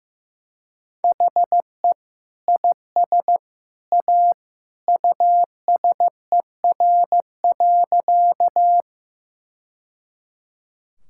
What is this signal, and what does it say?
Morse code, heard as a keyed tone, about 15 wpm, 700 Hz. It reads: HE IS A USER.
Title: Sentences from Top 1000 Words - Review Part 5 - 15wpm